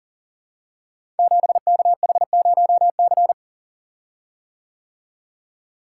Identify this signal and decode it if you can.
Morse code, heard as a keyed tone, about 42 wpm, 700 Hz. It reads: 7KH0C
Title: Code Groups - 5 Letter:Numbers - 40wpm